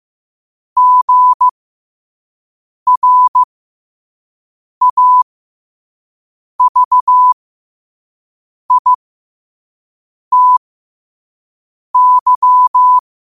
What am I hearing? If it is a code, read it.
Morse code, heard as a keyed tone, about 15 wpm, spaced as for 5 wpm, 1000 Hz. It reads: GRAVITY
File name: qotdfiles / 5wpm